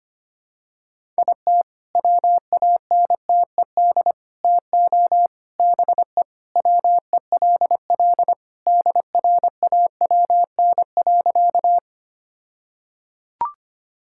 Morse code, heard as a keyed tone, about 25 wpm, 700 Hz. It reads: IT WANTED TO BE WELL DRAWN.
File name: practice / 25wpm / All-Age Appropriate Jokes - 25wpm